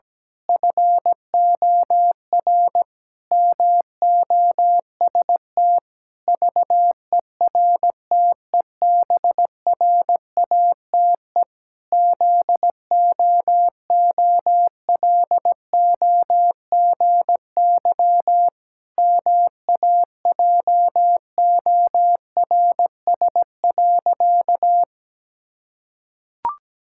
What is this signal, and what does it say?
Morse code, heard as a keyed tone, about 17 wpm, 700 Hz. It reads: FOR MOST VERTEBRATE ZOOLOGY MAJORS.
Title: Pangram Challenge v2 - 17wpm